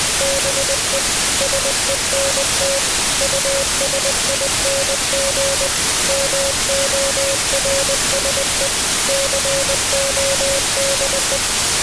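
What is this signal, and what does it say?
Morse code, heard as a keyed tone, about 20 wpm, 550 Hz, with radio noise. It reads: BE SENT USING MORSE CODE
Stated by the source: Morse code signal mixed with noise.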